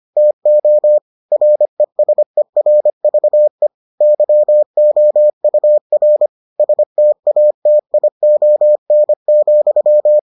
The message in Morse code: TO RESERVE YOUR STATION,

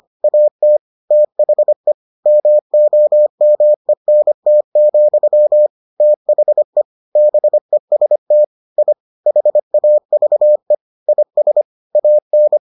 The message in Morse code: AT THE MOMENT, THE BEST I HAVE IS AN

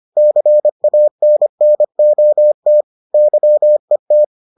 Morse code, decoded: CANNOT YET

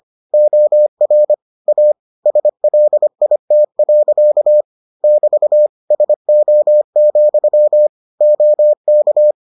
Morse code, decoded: OR A SLIT. <BT> SO, OK

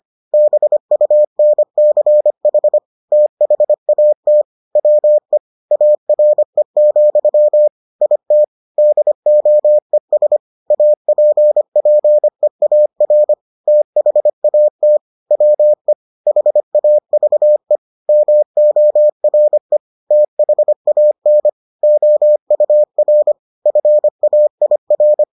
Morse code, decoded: BUNCH THAT WE ARE, IT DOES APPEAR THAT WE HAVE MORE THAN OUR FAIR

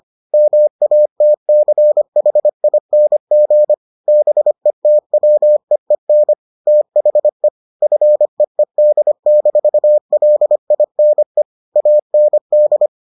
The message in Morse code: MATCHING BETWEEN THE FEED-LINE AND

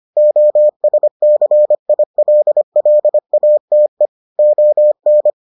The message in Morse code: OSCILLATE ON

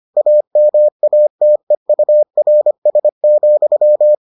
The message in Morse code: AMATEURS,

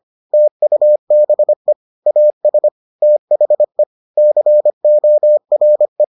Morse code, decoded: TUBE AS THE CORE

